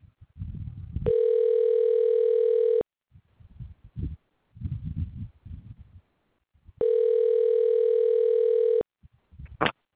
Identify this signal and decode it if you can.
Ringback tone (North America), 440 + 480 Hz.